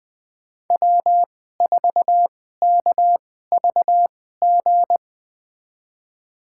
Morse code, heard as a keyed tone, about 20 wpm, spaced as for 14 wpm, 700 Hz. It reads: W4KVG
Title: USA Call Signs - 2x Character Spacing - 20wpm